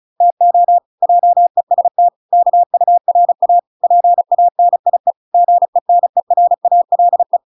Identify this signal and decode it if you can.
Morse code, heard as a keyed tone, about 34 wpm, 700 Hz. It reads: TO JEST KURA PANIE GENERALE